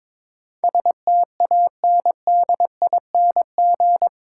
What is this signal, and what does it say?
Morse code, heard as a keyed tone, about 22 wpm, 700 Hz. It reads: STANDING
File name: practice / 22wpm / Eight Letter Words - 22wpm